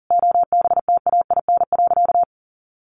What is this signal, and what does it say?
Morse code, heard as a keyed tone, about 40 wpm, 700 Hz. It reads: OBTAIN.
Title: Sentences from Top 1300 Words 40wpm